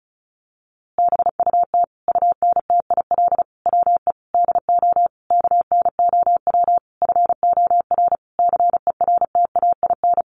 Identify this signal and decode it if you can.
Morse code, heard as a keyed tone, about 34 wpm, 700 Hz. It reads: BUT UNTIL WE DO KNOW FOR CERTAIN
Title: A Sound Of Thunder 35wpm